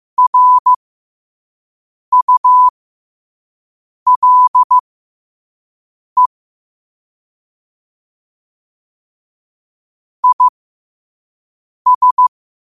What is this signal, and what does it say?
Morse code, heard as a keyed tone, about 15 wpm, spaced as for 5 wpm, 1000 Hz. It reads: RULE IS
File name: qotdfiles / 5wpm